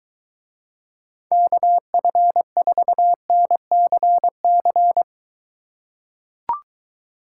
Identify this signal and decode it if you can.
Morse code, heard as a keyed tone, about 23 wpm, 700 Hz. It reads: KF4NCC